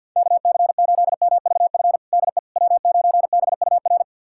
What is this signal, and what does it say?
Morse code, heard as a keyed tone, about 52 wpm, 700 Hz. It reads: KC8MVF DE W9BUR
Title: USA Callsign DE Callsign 50wpm